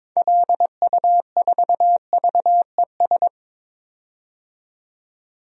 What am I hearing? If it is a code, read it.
Morse code, heard as a keyed tone, about 23 wpm, 700 Hz. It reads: LU4VES
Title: DXCC-100 Call Signs - 6 Characters - 22wpm